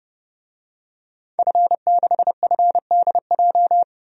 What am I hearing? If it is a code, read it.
Morse code, heard as a keyed tone, about 29 wpm, 700 Hz. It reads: F6FDJ
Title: DXCC-100 Call Signs - 5 Characters - 30wpm